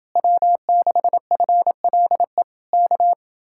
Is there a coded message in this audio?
Yes: Morse code from 0.2 to 3.1 s, heard as a keyed tone, about 30 wpm, 700 Hz. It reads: W6FLE K